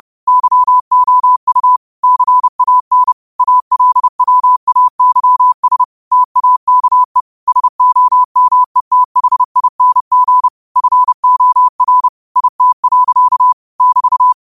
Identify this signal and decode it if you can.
Morse code, heard as a keyed tone, about 30 wpm, 1000 Hz. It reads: YOU CAN ALWAYS TAKE SOMETHING FOR IT. <BT>